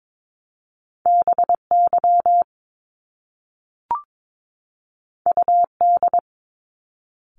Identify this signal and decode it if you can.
Morse code, heard as a keyed tone, about 22 wpm, 700 Hz. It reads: BY UD
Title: Top Two Letter Combinations in Words 22wpm